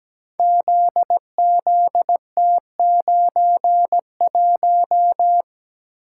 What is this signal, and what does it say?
Morse code, heard as a keyed tone, about 17 wpm, 700 Hz. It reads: ZZT91
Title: Code Groups - 5 Letter:Numbers - 17wpm